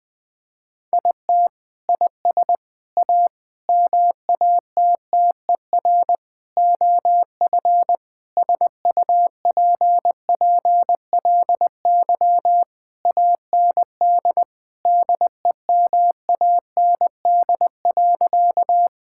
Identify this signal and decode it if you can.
Morse code, heard as a keyed tone, about 20 wpm, 700 Hz. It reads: IT IS A MATTER OF SUPPLY AND DEMAND.
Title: Sentences from Top 1000 Words - Review Part 5 - 20wpm